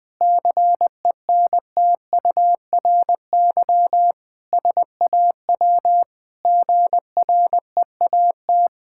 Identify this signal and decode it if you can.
Morse code, heard as a keyed tone, about 21 wpm, 700 Hz. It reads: CENTURY SAW GREAT
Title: Sentences with 5-Word Phrases - 20wpm